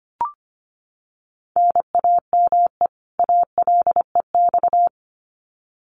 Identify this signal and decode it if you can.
Morse code, heard as a keyed tone, about 25 wpm, 700 Hz. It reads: NAME ALEX
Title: Common QSO Phrases 25wpm